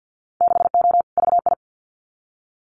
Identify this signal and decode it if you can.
Morse code, heard as a keyed tone, about 50 wpm, 700 Hz. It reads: 6O VI
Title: Winter Field Day Exchanges Encore 50wpm